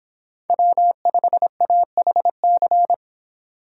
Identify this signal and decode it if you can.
Morse code, heard as a keyed tone, about 26 wpm, 700 Hz. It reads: W5AHC